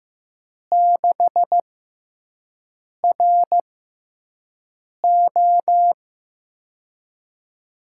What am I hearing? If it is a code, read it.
Morse code, heard as a keyed tone, about 15 wpm, spaced as for 5 wpm, 700 Hz. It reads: 6RO
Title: USA Call Signs - 5 Characters - 6x Character Spacing - 15wpm